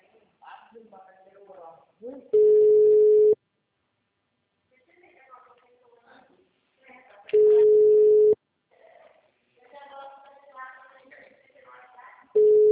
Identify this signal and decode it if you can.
Ringback tone (Europe, 425 Hz), with speech.